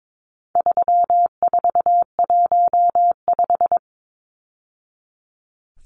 Morse code, heard as a keyed tone, about 22 wpm, 700 Hz. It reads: 3415
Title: Numbers - 4-Digit 22wpm